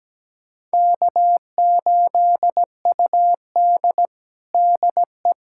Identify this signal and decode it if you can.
Morse code, heard as a keyed tone, about 17 wpm, 700 Hz. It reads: K8UD DE
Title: USA Callsign DE Callsign - 4 Characters - 17wpm